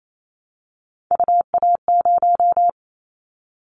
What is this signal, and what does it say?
Morse code, heard as a keyed tone, about 28 wpm, 700 Hz. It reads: UA0
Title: Callsign Prefixes for Top 100 DXCC - 28wpm